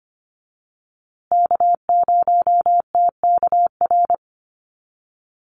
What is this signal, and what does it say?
Morse code, heard as a keyed tone, about 25 wpm, 700 Hz. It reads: K0TKR
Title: USA Call Signs Repeated 2x 25wpm